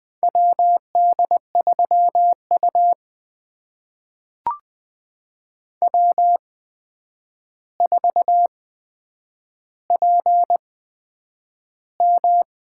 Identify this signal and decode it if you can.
Morse code, heard as a keyed tone, about 20 wpm, spaced as for 9 wpm, 700 Hz. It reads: WD3U W4PM
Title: USA Call Signs - 4 Characters - 8x Character Spacing - 20wpm